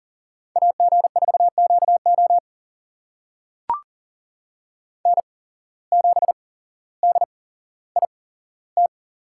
Morse code, heard as a keyed tone, about 40 wpm, spaced as for 17 wpm, 700 Hz. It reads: AG4QO N7DIT
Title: USA Call Signs - 5 Characters - 8x Character Spacing - 40wpm